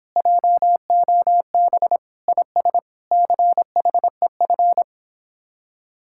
Morse code, heard as a keyed tone, about 26 wpm, 700 Hz. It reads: JOB IS CHEF